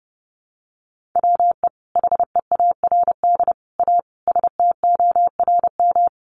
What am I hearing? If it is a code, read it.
Morse code, heard as a keyed tone, about 30 wpm, 700 Hz. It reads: WE HEARD A STORM
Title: Sentences from Top 1500 Words 30wpm